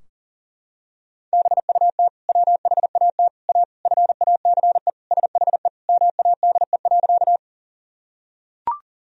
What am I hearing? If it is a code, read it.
Morse code, heard as a keyed tone, about 40 wpm, 700 Hz. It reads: BUT WHAT A FACE SHE MADE.
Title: Sentences from Top 1000 Words - Review Part 2 - 40wpm